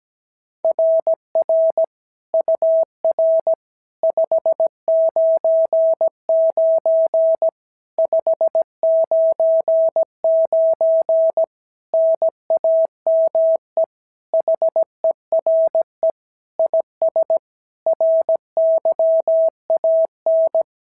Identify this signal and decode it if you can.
Morse code, heard as a keyed tone, about 16 wpm, 650 Hz. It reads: RR UR 599 599 NAME HERE IS RYAN
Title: SOTA QSO Practice 17wpm